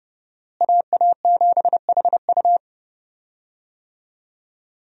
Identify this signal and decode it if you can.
Morse code, heard as a keyed tone, about 30 wpm, 700 Hz. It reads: AA7HU
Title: USA Call Signs - 5 Characters - Repeated 2x - 30wpm